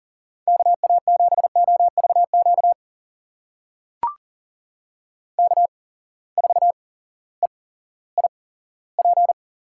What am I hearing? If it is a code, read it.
Morse code, heard as a keyed tone, about 40 wpm, spaced as for 18 wpm, 700 Hz. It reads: KA7OVQ K4EIP
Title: USA Call Signs - 8x Character Spacing - 40wpm